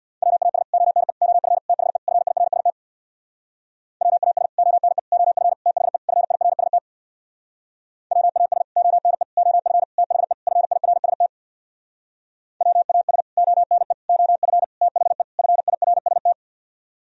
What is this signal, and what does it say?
Morse code, heard as a keyed tone, about 60 wpm, 700 Hz. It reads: WAS ONE OF THE FIRST WAS ONE OF THE FIRST WAS ONE OF THE FIRST WAS ONE OF THE FIRST
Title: Speed-Racing - Sets of 5 Words 50wpm